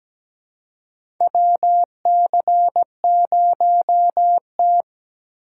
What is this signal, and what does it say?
Morse code, heard as a keyed tone, about 17 wpm, 700 Hz. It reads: WC0T
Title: USA Call Signs - 4 Characters - 4x Character Spacing - 17wpm